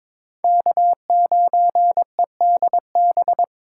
Morse code, heard as a keyed tone, about 26 wpm, 700 Hz. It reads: K9EDB
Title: Speed Racing - USA 1x3 Call Signs 22wpm